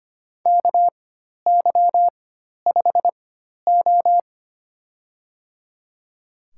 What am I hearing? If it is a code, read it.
Morse code, heard as a keyed tone, about 25 wpm, spaced as for 12 wpm, 700 Hz. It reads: KY5O
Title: USA Call Signs - 4 Characters - 4x Character Spacing - 25wpm